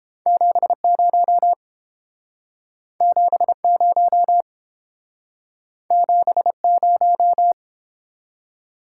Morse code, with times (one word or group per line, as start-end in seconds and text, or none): 0.3-1.5 70
3.0-4.4 70
5.9-7.5 70